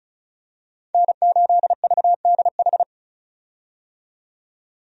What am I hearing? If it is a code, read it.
Morse code, heard as a keyed tone, about 36 wpm, 700 Hz. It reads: N8VDH
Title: USA Call Signs - 5 Characters - 35wpm